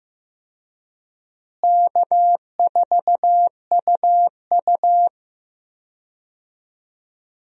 Morse code, heard as a keyed tone, about 16 wpm, 700 Hz. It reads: K4UU
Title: USA Call Signs - 4 Characters - Repeated 2x - 15wpm